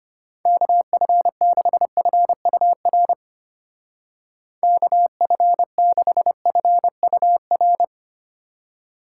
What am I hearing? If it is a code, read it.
Morse code, heard as a keyed tone, about 31 wpm, 700 Hz. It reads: KF6FUR KF6FUR
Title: Speed Racing - USA 2x3 Call Signs 25wpm